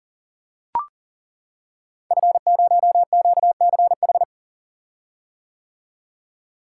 Morse code, heard as a keyed tone, about 42 wpm, 690 Hz. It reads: F0QCH
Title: Code Groups - 5 Letter:Numbers - 40wpm